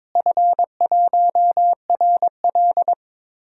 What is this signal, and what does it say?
Morse code, heard as a keyed tone, about 22 wpm, 700 Hz. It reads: F1RL
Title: DXCC-100 Call Signs - 4 Characters - Repeated 2x - 22wpm